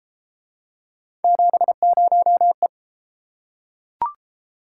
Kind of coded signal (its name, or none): Morse code